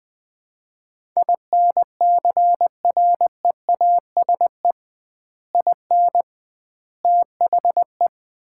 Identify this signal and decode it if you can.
Morse code, heard as a keyed tone, about 20 wpm, 700 Hz. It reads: INCREASE IN THE
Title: 2x Word-Spacing - Sets of 5 Words - 20wpm